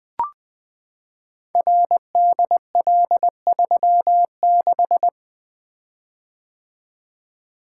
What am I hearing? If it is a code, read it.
Morse code, heard as a keyed tone, about 20 wpm, 700 Hz. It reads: RDL36
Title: Code Groups - 5 Letter:Numbers - 20wpm